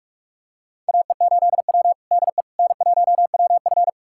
Morse code, heard as a keyed tone, about 42 wpm, 700 Hz. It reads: AE8W DE N1WF
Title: USA Callsign DE Callsign - 4 Characters - 45wpm